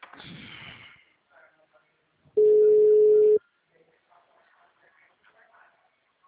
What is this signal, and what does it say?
Ringback tone (Europe, 425 Hz).
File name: out- -2007- - - .21830